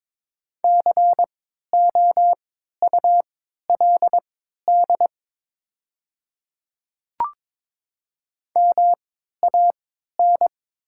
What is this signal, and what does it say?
Morse code, heard as a keyed tone, about 22 wpm, spaced as for 13 wpm, 690 Hz. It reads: COULD MAN
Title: 3x Character Spacing - Top 200 Words - 22wpm